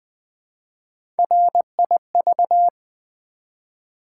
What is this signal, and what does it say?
Morse code, heard as a keyed tone, about 20 wpm, 700 Hz. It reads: RIV
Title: Build A Word - 1 Letter At A Time - 5-Letter Words - 20wpm